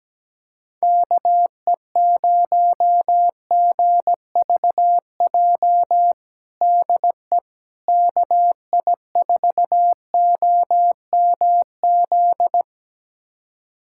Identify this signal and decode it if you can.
Morse code, heard as a keyed tone, about 17 wpm, 700 Hz. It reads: KE0GVJ DE KI4OMZ
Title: USA Callsign DE Callsign 17wpm